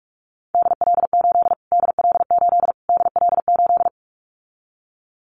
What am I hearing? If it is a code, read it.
Morse code, heard as a keyed tone, about 44 wpm, 700 Hz. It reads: DL8 DL8 DL8
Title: Callsign Prefixes for Top 25 DXCC - Repeated 3x - 45wpm